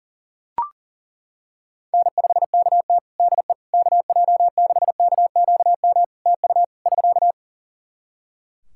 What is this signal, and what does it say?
Morse code, heard as a keyed tone, about 37 wpm, 700 Hz. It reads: N5KT DE KJ6KQM TU <SK>
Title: Practice QSO - Exchange 4 - Ending - 40wpm